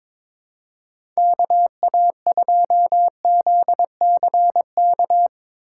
Morse code, heard as a keyed tone, about 22 wpm, 700 Hz. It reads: KA2ZCK
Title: USA Call Signs - 6 Characters - 3x Character Spacing - 22wpm